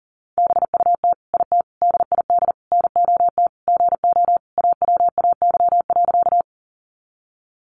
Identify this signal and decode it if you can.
Morse code, heard as a keyed tone, about 40 wpm, 700 Hz. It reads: BUT IT DID NOT GO AWAY.
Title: Sentences from Top 600 Words 40wpm